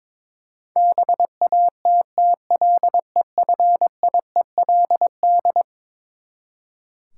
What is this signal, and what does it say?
Morse code, heard as a keyed tone, about 22 wpm, 700 Hz. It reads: BATTLEFIELD
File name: practice / 22wpm / Eleven Letter Words - 22wpm